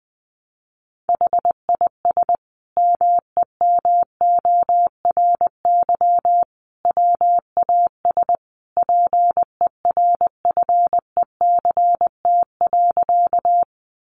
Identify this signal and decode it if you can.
Morse code, heard as a keyed tone, about 20 wpm, 700 Hz. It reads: HIS MEMORY WAS PERFECT.
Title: Sentences from Top 1300 Words 20wpm